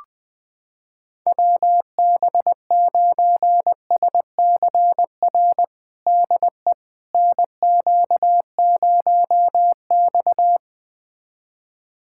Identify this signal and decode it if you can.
Morse code, heard as a keyed tone, about 20 wpm, 700 Hz. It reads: WB9SCR DE NQ0X
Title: USA Callsign DE Callsign 20wpm